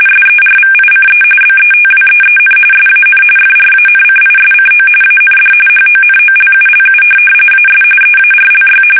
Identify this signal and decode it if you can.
RTTY, about 45.45 baud. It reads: SERVICE BETWEEN SAN FRANCISCO AND HONOLULU AS EARLY
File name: fsk850 / teletype-RTTY